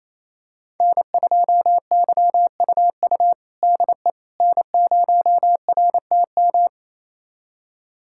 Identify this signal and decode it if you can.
Morse code, heard as a keyed tone, about 28 wpm, 700 Hz. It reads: N2YUU DE N0RTM
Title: USA Callsign DE Callsign - 5 Characters - 28wpm